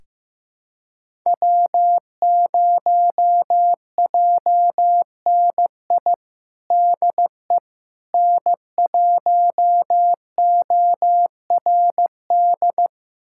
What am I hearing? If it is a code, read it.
Morse code, heard as a keyed tone, about 15 wpm, 700 Hz. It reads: W0JNI DE N1ORD
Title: USA Callsign DE Callsign - 5 Characters - 15wpm